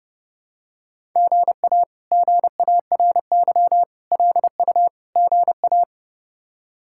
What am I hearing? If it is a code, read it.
Morse code, heard as a keyed tone, about 30 wpm, 700 Hz. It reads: GA GARY LU GA
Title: K1USN SST QSO Practice - 30wpm